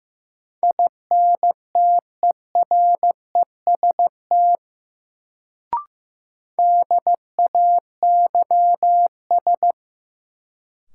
Morse code, heard as a keyed tone, about 15 wpm, 700 Hz. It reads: INTEREST DAYS
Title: Sets of 5 Words Encore - New Word Drill - 15wpm